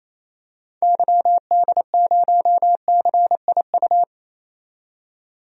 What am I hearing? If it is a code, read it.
Morse code, heard as a keyed tone, about 28 wpm, 700 Hz. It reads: YD0CIU
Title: DXCC-100 Call Signs - 6 Characters - 28wpm